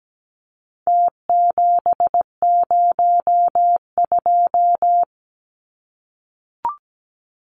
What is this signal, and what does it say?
Morse code, heard as a keyed tone, about 17 wpm, 700 Hz. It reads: T702
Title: Numbers - Contest Style - 4-Digit 17wpm